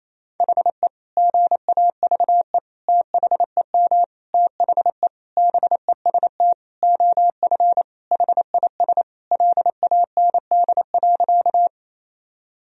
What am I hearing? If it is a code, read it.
Morse code, heard as a keyed tone, about 28 wpm, 700 Hz. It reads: HE GAVE THEM THE BEST OF HIS LAND.
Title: Sentences from Top 1000 Words - Review Part 1 - 28wpm